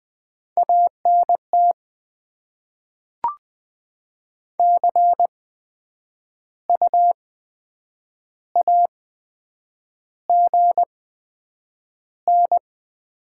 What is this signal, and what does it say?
Morse code, heard as a keyed tone, about 20 wpm, 700 Hz. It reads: A N T C U A G N
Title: Top 20 QSO Elements - 8x Character Spacing - 20wpm